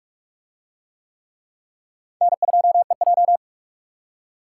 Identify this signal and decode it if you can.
Morse code, heard as a keyed tone, about 43 wpm, 700 Hz. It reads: N2EJ